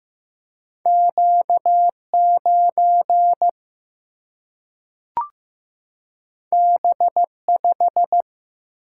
Morse code, heard as a keyed tone, about 15 wpm, 700 Hz. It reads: Q 9 B 5
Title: Single-Letter Single-Number 15wpm